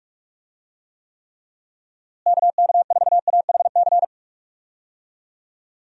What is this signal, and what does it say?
Morse code, heard as a keyed tone, about 46 wpm, 700 Hz. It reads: KK4AHC
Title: USA Call Signs - 6 Characters - Repeated 3x - 45wpm